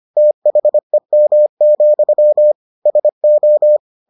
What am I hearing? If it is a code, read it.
Morse code, heard as a keyed tone, about 25 wpm, 600 Hz. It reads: THEM, SO